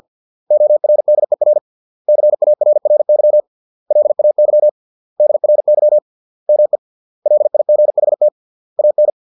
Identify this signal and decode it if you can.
Morse code, heard as a keyed tone, about 50 wpm, 600 Hz. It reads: ORDER CARRY LAY DRY ME LIGHT AN